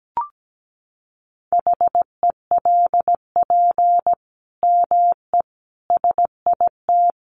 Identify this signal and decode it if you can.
Morse code, heard as a keyed tone, about 17 wpm, 700 Hz. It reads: HELP ME SIT
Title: Sentences from Top 500 Words 17wpm